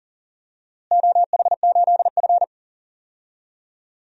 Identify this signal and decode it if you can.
Morse code, heard as a keyed tone, about 40 wpm, 700 Hz. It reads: OH8F
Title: DXCC-100 Call Signs - 4 Characters - Repeated 3x - 40wpm